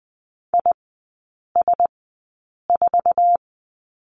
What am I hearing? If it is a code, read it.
Morse code, heard as a keyed tone, about 20 wpm, spaced as for 8 wpm, 700 Hz. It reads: IS4
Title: 047 Lesson 9e - Number 4 - Three Characters 20wpm